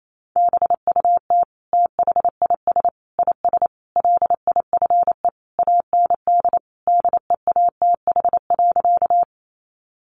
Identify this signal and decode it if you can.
Morse code, heard as a keyed tone, about 28 wpm, 700 Hz. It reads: BUT THIS IS LIFE AND DEATH.
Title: Sentences from Top 1100 Words 28wpm